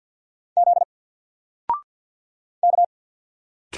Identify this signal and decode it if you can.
Morse code, heard as a keyed tone, about 48 wpm, 700 Hz. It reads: Z K
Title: Single Character (Letter:Number:Punctuation:Prosign) Alt 50wpm